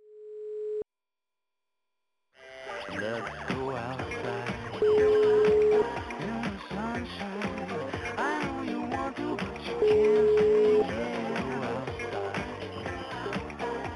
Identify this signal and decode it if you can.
Ringback tone (Europe, 425 Hz), with noise.